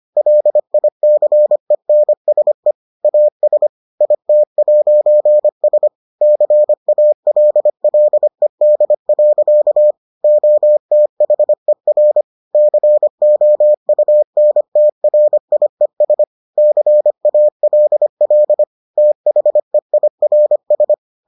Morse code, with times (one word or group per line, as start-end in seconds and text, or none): 0.2-2.7 LICENSE
3.0-3.7 AS
4.0-5.9 IT'S
6.2-9.9 CALLED.
10.2-12.2 OTHER
12.5-16.2 COUNTRIES
16.6-18.6 CALL
19.0-20.9 THEIRS